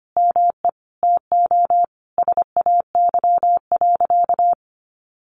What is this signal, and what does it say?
Morse code, heard as a keyed tone, about 25 wpm, 700 Hz. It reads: ME TO SAY.
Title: Sentences from Top 600 Words 25wpm